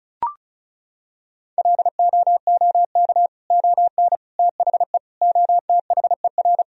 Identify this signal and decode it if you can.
Morse code, heard as a keyed tone, about 36 wpm, 700 Hz. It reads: LOOK ON THE OTHER
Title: Sentences from Top 1000 Words - Review Part 1 - 35wpm